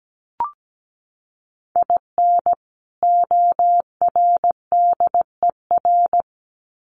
Morse code, heard as a keyed tone, about 17 wpm, 700 Hz. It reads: IN ORDER
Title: Sets of 2 Words 17wpm